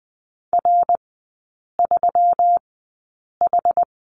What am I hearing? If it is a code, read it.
Morse code, heard as a keyed tone, about 20 wpm, spaced as for 8 wpm, 700 Hz. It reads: R3H
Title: 184 Lesson 30e - Letter b - Three Characters 20wpm